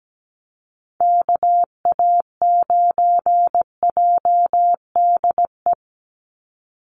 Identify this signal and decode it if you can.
Morse code, heard as a keyed tone, about 17 wpm, 700 Hz. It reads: KA9JDE
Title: USA Call Signs 17wpm